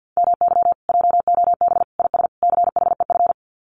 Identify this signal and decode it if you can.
Morse code, heard as a keyed tone, about 50 wpm, 700 Hz. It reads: MY JOB IS CHEF
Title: Common QSO Phrases Encore 50wpm